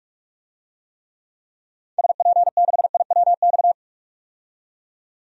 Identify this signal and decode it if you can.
Morse code, heard as a keyed tone, about 47 wpm, 700 Hz. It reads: SP6IWX
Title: DXCC-100 Call Signs - 6 Characters - 45wpm